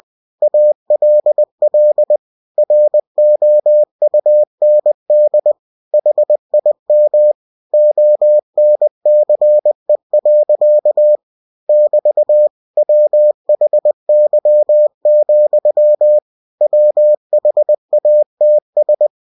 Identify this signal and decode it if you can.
Morse code, heard as a keyed tone, about 20 wpm, 600 Hz. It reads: ALL ROUND HIM ONCE. <BT> WHY, WHATS
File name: cw-books / winnie-the-pooh / winnie-20wpm-page-0026